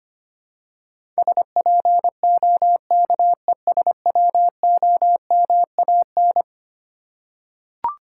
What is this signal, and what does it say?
Morse code, heard as a keyed tone, about 25 wpm, 700 Hz. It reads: SPOKESWOMAN
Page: morse-code-ninja-practice-25wpm